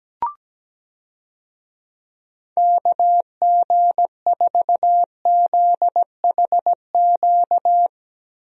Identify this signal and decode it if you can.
Morse code, heard as a keyed tone, about 17 wpm, 700 Hz. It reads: KG4ZHQ
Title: USA Call Signs - 6 Characters - Repeated 3x - 17wpm